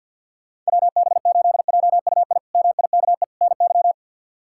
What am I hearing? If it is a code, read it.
Morse code, heard as a keyed tone, about 53 wpm, 690 Hz. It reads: WB8JUI MIKE NY
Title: K1USN SST QSO Practice - 50wpm